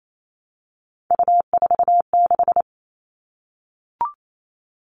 Morse code, heard as a keyed tone, about 28 wpm, 700 Hz. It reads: U46
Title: All Cut Numbers - 3-Digit 28wpm